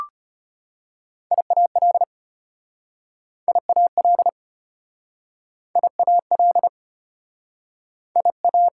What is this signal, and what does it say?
Morse code, heard as a keyed tone, about 33 wpm, 690 Hz. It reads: IAL IAL IAL IA